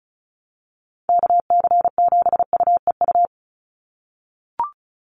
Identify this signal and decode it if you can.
Morse code, heard as a keyed tone, about 36 wpm, 700 Hz. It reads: KC7UEU
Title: USA Call Signs 35wpm